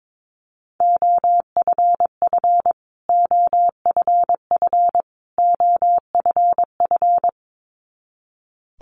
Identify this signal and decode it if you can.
Morse code, heard as a keyed tone, about 22 wpm, 700 Hz. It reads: OFF OFF OFF
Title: Top 200 Words Repeated 22wpm